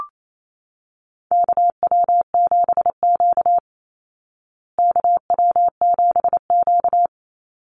Morse code, heard as a keyed tone, about 28 wpm, 700 Hz. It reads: KW7Q KW7Q
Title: Call Signs Repeated 2x - 28wpm